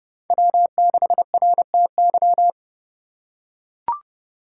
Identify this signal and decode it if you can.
Morse code, heard as a keyed tone, about 30 wpm, 700 Hz. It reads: W6RTY
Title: USA Call Signs - 5 Characters - 3x Character Spacing - 30wpm